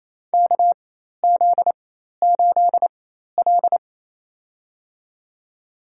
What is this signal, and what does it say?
Morse code, heard as a keyed tone, about 28 wpm, spaced as for 13 wpm, 700 Hz. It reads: KZ8L
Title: USA Call Signs - 4 Characters - 4x Character Spacing - 28wpm